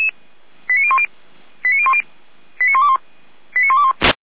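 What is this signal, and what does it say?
Five-tone selective call (ZVEI), heard as quick runs of five tones, tones of about 70 ms. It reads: 89010 89010 89121 89121
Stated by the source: Five-tone ZVEI selective call received on an SDR.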